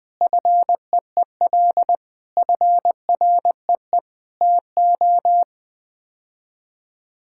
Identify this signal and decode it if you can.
Morse code, heard as a keyed tone, about 20 wpm, 700 Hz. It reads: FEEL FREE TO
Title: Sets of 8 Words - 20wpm